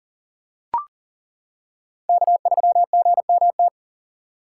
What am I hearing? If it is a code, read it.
Morse code, heard as a keyed tone, about 40 wpm, 690 Hz. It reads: K3GMT
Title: K1USN SST QSO Practice - 40wpm